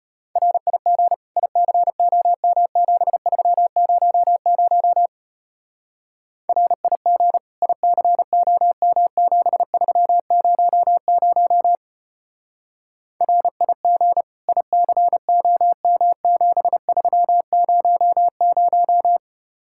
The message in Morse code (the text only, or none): RIG ICOM7300 RIG ICOM7300 RIG ICOM7300